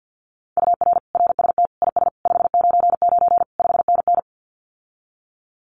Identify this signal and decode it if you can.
Morse code, heard as a keyed tone, about 50 wpm, 700 Hz. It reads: UR RST IS 599 5NN
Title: Common QSO Phrases Encore 50wpm